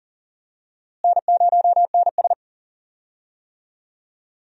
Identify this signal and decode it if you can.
Morse code, heard as a keyed tone, about 37 wpm, 700 Hz. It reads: N0NS
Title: USA Call Signs - 4 Characters - Repeated 2x - 40wpm